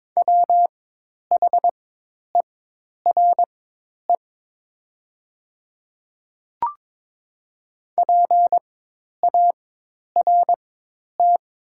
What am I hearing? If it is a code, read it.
Morse code, heard as a keyed tone, about 22 wpm, spaced as for 10 wpm, 700 Hz. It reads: WHERE PART
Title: 4x Character Spacing - Top 200 Words - 22wpm